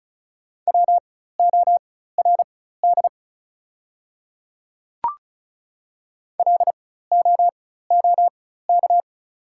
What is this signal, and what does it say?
Morse code, heard as a keyed tone, about 36 wpm, spaced as for 16 wpm, 690 Hz. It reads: WORD LOOK
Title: 4x Character Spacing - Top 200 Words - 35wpm